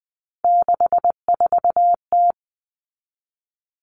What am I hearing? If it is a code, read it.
Morse code, heard as a keyed tone, about 20 wpm, 700 Hz. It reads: 64T
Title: Three Challenging Letter-Number Combinations 20wpm